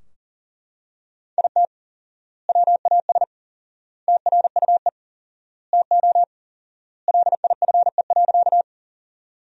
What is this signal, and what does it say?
Morse code, heard as a keyed tone, about 40 wpm, 700 Hz. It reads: IT WAS TRUE TO LIFE.
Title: 4x Word-Spacing - Sentences from Top 400 Words - 40wpm